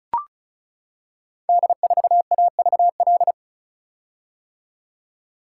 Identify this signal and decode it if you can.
Morse code, heard as a keyed tone, about 36 wpm, 700 Hz. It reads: D4AVL
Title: Code Groups - 5 Letter:Numbers - 35wpm